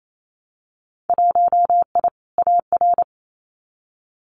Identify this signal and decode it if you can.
Morse code, heard as a keyed tone, about 28 wpm, 700 Hz. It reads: 1I AR
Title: Winter Field Day Exchanges Encore 28wpm